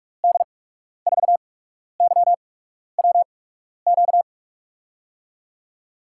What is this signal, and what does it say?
Morse code, heard as a keyed tone, about 42 wpm, spaced as for 12 wpm, 700 Hz. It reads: D4YWQ
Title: USA Call Signs - 8x Character Spacing - 45wpm